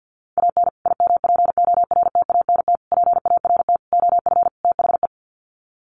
Morse code, heard as a keyed tone, about 50 wpm, 700 Hz. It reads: AN IMPORTANT PART OF THE
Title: Sets of 5 Words Encore 50wpm